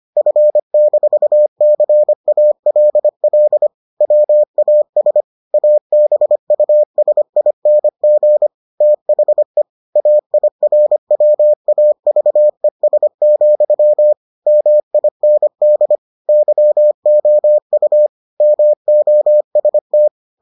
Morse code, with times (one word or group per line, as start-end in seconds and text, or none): 0.2-3.7 F-CALL
4.0-5.2 WAS
5.5-8.5 ABUSING
8.8-9.6 THE
10.0-14.1 AIRWAVES,
14.5-16.0 MIND
16.3-18.1 YOU
18.4-20.1 MOST